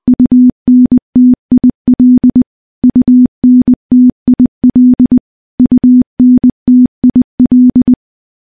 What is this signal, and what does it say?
Morse code, heard as a keyed tone, about 21 wpm, 250 Hz. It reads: UNTIL UNTIL UNTIL
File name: morse-training-20WPM